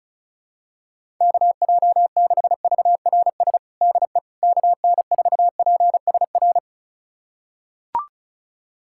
Morse code, heard as a keyed tone, about 34 wpm, 700 Hz. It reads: KJ6VRS DE KN4PSR
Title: USA Callsign DE Callsign - 6 Characters - 35wpm